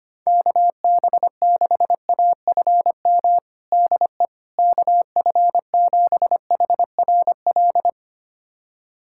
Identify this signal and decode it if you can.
Morse code, heard as a keyed tone, about 25 wpm, 700 Hz. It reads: KB6AFM DE KF7HRL
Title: USA Callsign DE Callsign - 6 Characters - 25wpm